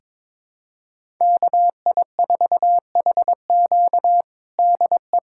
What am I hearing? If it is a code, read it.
Morse code, heard as a keyed tone, about 21 wpm, 700 Hz. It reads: KI4HQ DE
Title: USA Callsign DE Callsign 22wpm